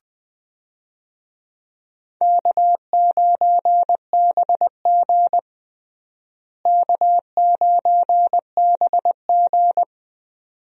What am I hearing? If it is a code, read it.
Morse code, heard as a keyed tone, about 20 wpm, 700 Hz. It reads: K9BG K9BG
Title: USA Call Signs - 4 Characters - Repeated 2x - 20wpm